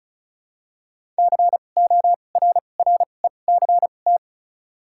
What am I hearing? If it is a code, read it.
Morse code, heard as a keyed tone, about 36 wpm, spaced as for 25 wpm, 700 Hz. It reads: CORRECT
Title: 2x Character Spacing - Top 500 Words - 35wpm